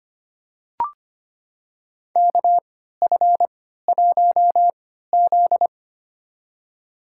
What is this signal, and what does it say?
Morse code, heard as a keyed tone, about 25 wpm, spaced as for 14 wpm, 700 Hz. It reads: KF1Z
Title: USA Call Signs - 3x Character Spacing - 25wpm